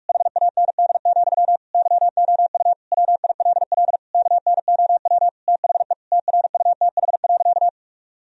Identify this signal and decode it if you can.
Morse code, heard as a keyed tone, about 46 wpm, 700 Hz. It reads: HAND, YOU WILL KNOW THE TRUTH.
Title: Sentences from Top 1000 Words - Review Part 3 - 45wpm